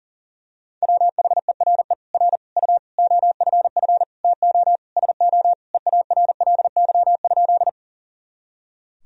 Morse code, heard as a keyed tone, about 40 wpm, 700 Hz. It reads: WHERE R U OFF TO SO EARLY?
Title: Sentences from Top 400 Words and Abbreviations 40wpm